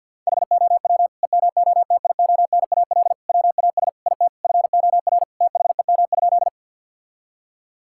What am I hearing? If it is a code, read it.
Morse code, heard as a keyed tone, about 50 wpm, 700 Hz. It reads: HOW EMOTIONAL WAS IT FOR THEM?
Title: Sentences from Top 1000 Words - Review Part 5 - 50wpm